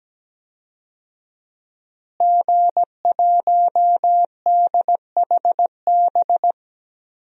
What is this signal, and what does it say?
Morse code, heard as a keyed tone, about 17 wpm, 700 Hz. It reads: G1DHB
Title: DXCC-100 Call Signs - 5 Characters - Repeated 3x - 17wpm